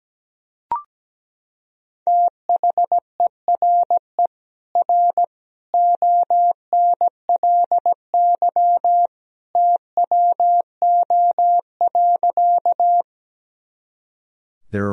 Morse code, heard as a keyed tone, about 17 wpm, 700 Hz. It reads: THERE R ONLY TWO.
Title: Sentences from Top 200 Words and Abbreviations 17wpm